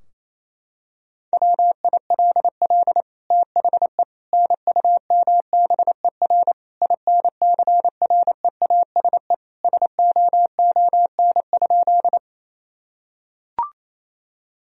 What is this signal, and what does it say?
Morse code, heard as a keyed tone, about 28 wpm, 700 Hz. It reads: WILL THE NUMBER INCREASE SOON?
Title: Sentences from Top 1000 Words - Review Part 4 - 28wpm